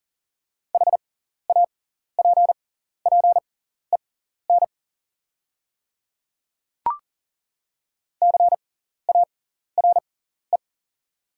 Morse code, heard as a keyed tone, about 39 wpm, spaced as for 14 wpm, 690 Hz. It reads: HAPPEN CARE
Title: 6x Character Spacing - Top 500 Words - 40wpm